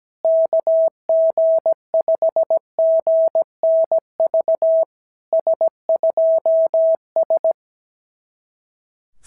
Morse code, heard as a keyed tone, about 17 wpm, 650 Hz. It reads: KG5GNV S2S
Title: SOTA QSO Practice 17wpm